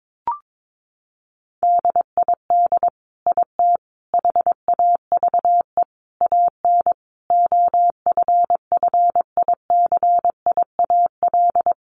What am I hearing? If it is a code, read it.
Morse code, heard as a keyed tone, about 22 wpm, 700 Hz. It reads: DID IT HAVE AN OFFICIAL
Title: Sentences from Top 1200 Words 22wpm